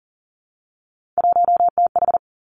Morse code, heard as a keyed tone, about 39 wpm, 700 Hz. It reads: 1TH
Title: Three Challenging Letter-Number Combinations 40wpm